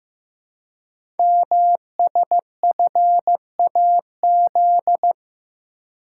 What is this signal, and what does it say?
Morse code, heard as a keyed tone, about 15 wpm, 700 Hz. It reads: MSFAZ
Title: Code Groups - 5 Letters - 15wpm